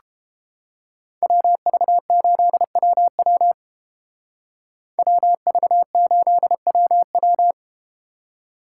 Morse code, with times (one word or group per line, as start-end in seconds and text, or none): 1.2-3.5 WV8WW
5.0-7.5 WV8WW